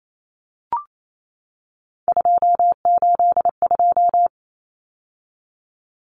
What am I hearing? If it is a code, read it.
Morse code, heard as a keyed tone, about 28 wpm, 700 Hz. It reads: 282
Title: Numbers - 3-Digit 28wpm